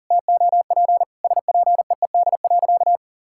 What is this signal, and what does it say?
Morse code, heard as a keyed tone, about 40 wpm, 700 Hz. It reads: TOP SPEED.
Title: Sentences from Top 1000 Words - Review Part 5 - 40wpm